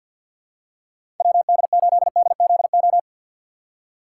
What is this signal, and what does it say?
Morse code, heard as a keyed tone, about 50 wpm, 700 Hz. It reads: WD8DZO
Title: USA Call Signs - 6 Characters - 4x Character Spacing - 50wpm